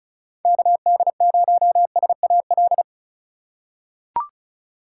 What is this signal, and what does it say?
Morse code, heard as a keyed tone, about 36 wpm, 700 Hz. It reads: KD0SAL
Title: USA Call Signs - 6 Characters - 3x Character Spacing - 35wpm